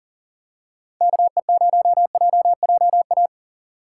Morse code, heard as a keyed tone, about 40 wpm, 700 Hz. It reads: KE0JJA
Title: USA Call Signs - 6 Characters - 6x Character Spacing - 40wpm